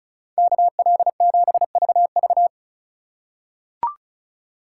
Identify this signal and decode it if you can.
Morse code, heard as a keyed tone, about 36 wpm, 690 Hz. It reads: KL7VV
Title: USA Call Signs - 5 Characters - 2x Character Spacing - 35wpm